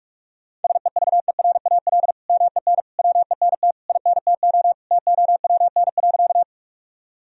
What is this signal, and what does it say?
Morse code, heard as a keyed tone, about 41 wpm, 700 Hz. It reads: SEVERAL MEN WENT INTO TOWN.
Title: Sentences from Top 1000 Words - Review Part 2 - 45wpm